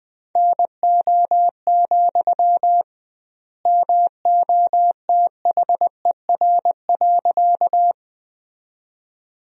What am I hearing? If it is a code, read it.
Morse code, heard as a keyed tone, about 20 wpm, 700 Hz. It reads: NO, MOTHER.
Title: 2x Word-Spacing - Sentences from Top 200 Words - 20wpm